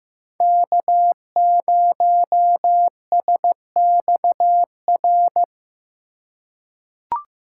Morse code, heard as a keyed tone, about 15 wpm, 700 Hz. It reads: K0SXR
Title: USA Call Signs - 5 Characters - 4x Character Spacing - 15wpm